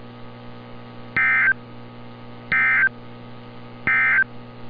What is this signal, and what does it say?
Emergency Alert System (SAME) tones, heard as harsh buzzing bursts. It is heard from 1.0 to 4.3 s.
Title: NOAA Weather Radio - Altoona, PA